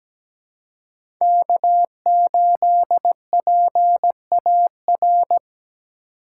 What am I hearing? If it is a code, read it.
Morse code, heard as a keyed tone, about 17 wpm, 700 Hz. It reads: K8PAR
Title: USA Call Signs - 5 Characters - 2x Character Spacing - 17wpm